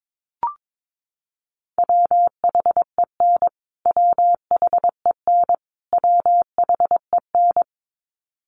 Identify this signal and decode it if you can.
Morse code, heard as a keyed tone, about 22 wpm, 700 Hz. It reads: WHEN WHEN WHEN
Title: Top 200 Words Repeated 22wpm